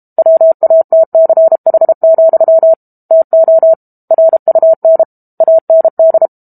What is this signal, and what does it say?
Morse code, heard as a keyed tone, about 32 wpm, 650 Hz. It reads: WATCH, TO RUN AND
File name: Book Two / 06-Chapter 6 - 32 WPM